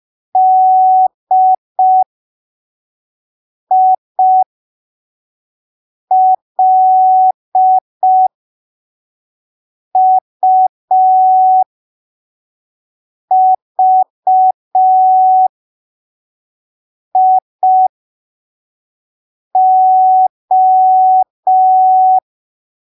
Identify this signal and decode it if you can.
Morse code, heard as a keyed tone, about 5 wpm, spaced as for 3 wpm, 750 Hz. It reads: DILUVIO
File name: arquivos / texto / texto 02 5ppm